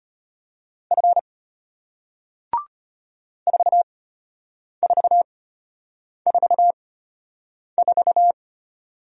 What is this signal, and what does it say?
Morse code, heard as a keyed tone, about 33 wpm, spaced as for 8 wpm, 700 Hz. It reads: F 4444